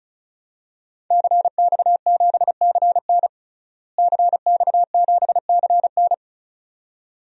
Morse code, heard as a keyed tone, about 36 wpm, 700 Hz. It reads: CX7CN CX7CN
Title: DXCC-100 Call Signs - 5 Characters - Repeated 2x - 35wpm